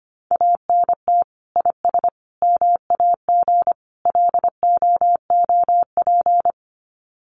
Morse code, heard as a keyed tone, about 25 wpm, 700 Hz. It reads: ANT IS MAG LOOP
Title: Common QSO Phrases 25wpm